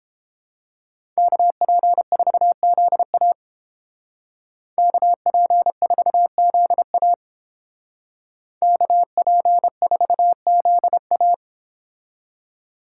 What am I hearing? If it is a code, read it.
Morse code, heard as a keyed tone, about 30 wpm, 700 Hz. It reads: KP4ZA KP4ZA KP4ZA